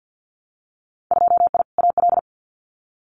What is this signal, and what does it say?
Morse code, heard as a keyed tone, about 50 wpm, 700 Hz. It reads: 2I AL
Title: Winter Field Day Exchanges 50wpm